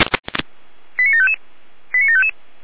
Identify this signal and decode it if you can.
Five-tone selective call (ZVEI), heard as quick runs of five tones, tones of about 70 ms. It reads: 89755 89755